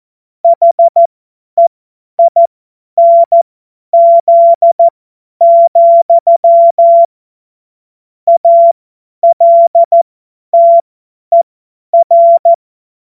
Morse code, heard as a keyed tone, about 14 wpm, spaced as for 10 wpm, 680 Hz. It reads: HEINZ, ALTER
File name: Morsekurs / Witze / Witz 17. Mai 2022 50 F70